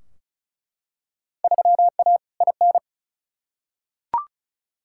Morse code, heard as a keyed tone, about 34 wpm, 700 Hz. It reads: 3A IN
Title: Field Day Exchanges - 35wpm